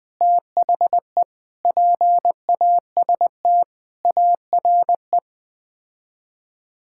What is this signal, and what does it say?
Morse code, heard as a keyed tone, about 20 wpm, 700 Hz. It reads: THE PAST ARE
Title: Sets of 8 Words - 20wpm